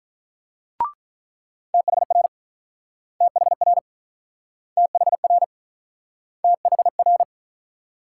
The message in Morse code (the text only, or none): THR THR THR THR